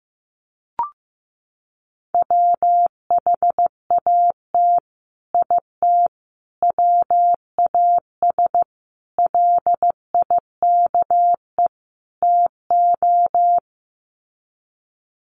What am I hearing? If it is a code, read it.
Morse code, heard as a keyed tone, about 15 wpm, 700 Hz. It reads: WHAT IT WAS LIKE TO
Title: Sets of 5 Words Encore 15wpm